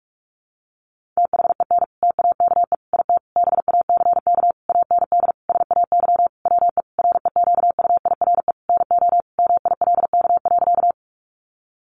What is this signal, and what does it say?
Morse code, heard as a keyed tone, about 44 wpm, 700 Hz. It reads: THEN TAKE IT BACK AND SAY WE REQUIRE NO MILK.
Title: Sentences from Top 1500 Words 45wpm